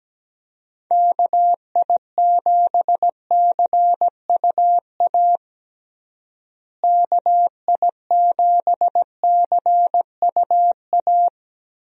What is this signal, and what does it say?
Morse code, heard as a keyed tone, about 17 wpm, 700 Hz. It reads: KI7CUA KI7CUA
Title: USA Call Signs - 6 Characters - Repeated 2x - 17wpm